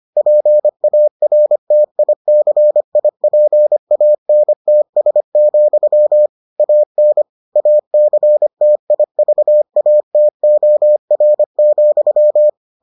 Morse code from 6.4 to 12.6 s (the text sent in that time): AN ACTIVATOR,